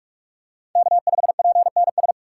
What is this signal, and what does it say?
Morse code, heard as a keyed tone, about 42 wpm, 700 Hz. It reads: K5PNS